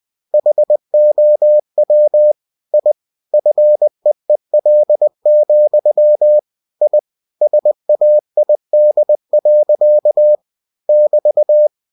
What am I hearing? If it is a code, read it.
Morse code, heard as a keyed tone, about 20 wpm, 600 Hz. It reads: HOW I FEEL, I SAID. <BT>
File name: cw-books / winnie-the-pooh / winnie-20wpm-page-0011